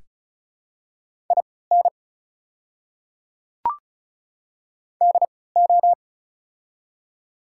Morse code, heard as a keyed tone, about 36 wpm, spaced as for 20 wpm, 690 Hz. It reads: IN DO
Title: 3x Character Spacing - Top 500 Words - 35wpm